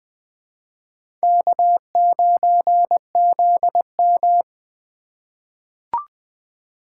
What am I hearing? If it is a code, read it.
Morse code, heard as a keyed tone, about 21 wpm, 700 Hz. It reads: K9ZM
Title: USA Call Signs - 4 Characters - 4x Character Spacing - 20wpm